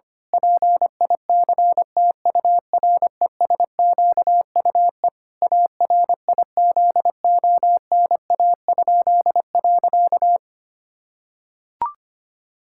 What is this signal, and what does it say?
Morse code, heard as a keyed tone, about 25 wpm, 700 Hz. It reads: PICTURESQUE ARIZONA?.
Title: Pangram Challenge v2 - 25wpm